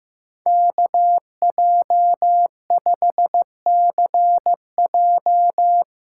Morse code, heard as a keyed tone, about 16 wpm, 700 Hz. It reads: KJ5CJ
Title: Speed Racing - USA Call Signs 15wpm